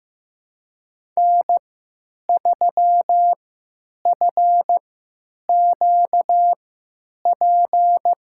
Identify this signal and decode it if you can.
Morse code, heard as a keyed tone, about 15 wpm, spaced as for 8 wpm, 700 Hz. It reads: N3FQP
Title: USA Call Signs - 5 Characters - 3x Character Spacing - 15wpm